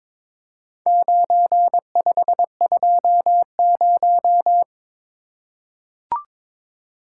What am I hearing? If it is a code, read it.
Morse code, heard as a keyed tone, about 22 wpm, 700 Hz. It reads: 9520
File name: practice / 22wpm / Four Letter-Number Combinations - 22wpm